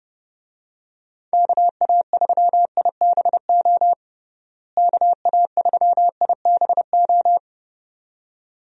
Morse code, heard as a keyed tone, about 30 wpm, 700 Hz. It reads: KA3IBO KA3IBO
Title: USA Call Signs - 6 Characters - Repeated 2x - 30wpm